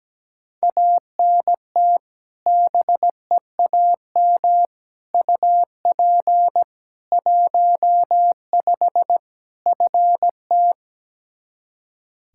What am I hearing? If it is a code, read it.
Morse code, heard as a keyed tone, about 17 wpm, 700 Hz. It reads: ANT BEAM UP 15 FT
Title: Practice QSO - Exchange 2 - Info - 17wpm